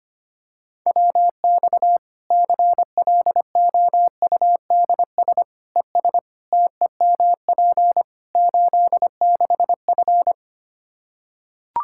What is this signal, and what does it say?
Morse code, heard as a keyed tone, about 25 wpm, 700 Hz. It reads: WX CLOUDS ES TEMP 86F
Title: Practice QSO - Exchange 2 - Info - 25wpm